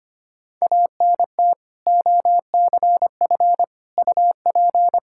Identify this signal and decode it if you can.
Morse code, heard as a keyed tone, about 25 wpm, 700 Hz. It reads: ANT OCF UP
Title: Practice QSO - Exchange 2 - Info - 25wpm